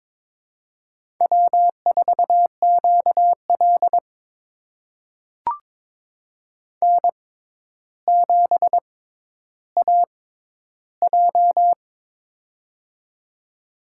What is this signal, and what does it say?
Morse code, heard as a keyed tone, about 23 wpm, spaced as for 11 wpm, 700 Hz. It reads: W4QL N7AJ
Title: USA Call Signs - 4 Characters - 6x Character Spacing - 22wpm